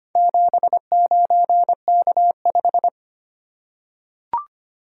Morse code, heard as a keyed tone, about 25 wpm, 700 Hz. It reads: 79K5
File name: practice / 25wpm / Four Letter-Number Combinations - 25wpm